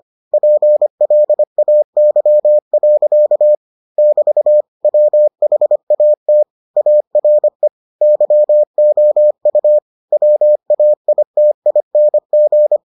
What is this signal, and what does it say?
Morse code, heard as a keyed tone, about 25 wpm, 600 Hz. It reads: PLAY. <BT> WHAT ARE YOU WAITING